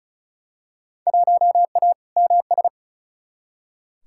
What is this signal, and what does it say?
Morse code, heard as a keyed tone, about 36 wpm, 700 Hz. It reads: 1A MS
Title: Field Day Exchanges - 35wpm